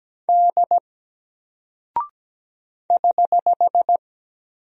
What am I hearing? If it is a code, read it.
Morse code, heard as a keyed tone, about 17 wpm, 700 Hz. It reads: D <HH>
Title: Single Character (Letter:Number:Punctuation:Prosign) 17wpm